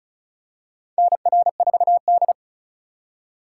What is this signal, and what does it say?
Morse code, heard as a keyed tone, about 36 wpm, 700 Hz. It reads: NR4D
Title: USA Call Signs - 2x Character Spacing - 35wpm